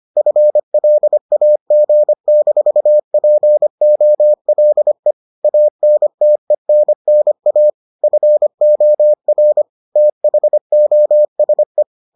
Morse code, heard as a keyed tone, about 25 wpm, 600 Hz. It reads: FLAG-POLE ANTENNA FOR THOSE